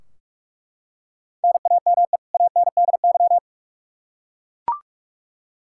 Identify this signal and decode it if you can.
Morse code, heard as a keyed tone, about 42 wpm, 700 Hz. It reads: NAME ANDY
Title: Practice QSO - Exchange 1 - Beginning - 45wpm